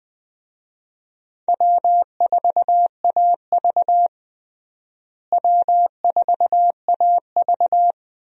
Morse code, heard as a keyed tone, about 21 wpm, 700 Hz. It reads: W4AV W4AV
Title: USA Call Signs - 4 Characters - Repeated 3x - 20wpm